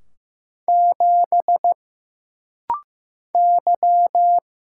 Morse code, heard as a keyed tone, about 15 wpm, 700 Hz. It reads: 7 Y
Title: Single Letter-Number Alt - Rapid-Fire 15wpm